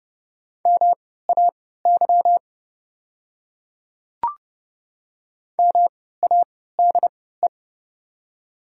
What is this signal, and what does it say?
Morse code, heard as a keyed tone, about 30 wpm, spaced as for 17 wpm, 700 Hz. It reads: MAY MADE
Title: 3x Character Spacing - Top 300 Words - 30wpm